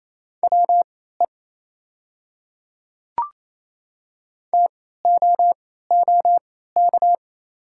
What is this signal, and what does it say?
Morse code, heard as a keyed tone, about 28 wpm, spaced as for 16 wpm, 700 Hz. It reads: WE TOOK
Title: 3x Character Spacing - Top 300 Words - 28wpm